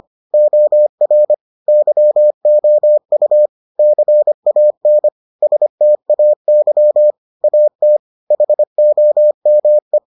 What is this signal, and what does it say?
Morse code, heard as a keyed tone, about 25 wpm, 600 Hz. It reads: OR YOU CAN STAY AT HOME